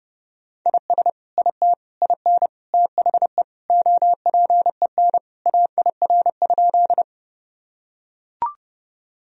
Morse code, heard as a keyed tone, about 30 wpm, 700 Hz. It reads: IS IT IN THE OPEN AIR?
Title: Sentences from Top 1000 Words - Review Part 1 - 30wpm